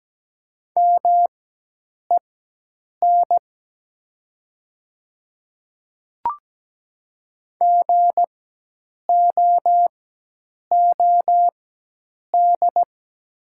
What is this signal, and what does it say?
Morse code, heard as a keyed tone, about 17 wpm, spaced as for 8 wpm, 700 Hz. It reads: MEN GOOD
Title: 4x Character Spacing - Top 200 Words - 17wpm